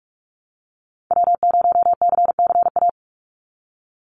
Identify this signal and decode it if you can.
Morse code, heard as a keyed tone, about 46 wpm, 700 Hz. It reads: W0CCA
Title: NA SOTA Activator Callsigns 45wpm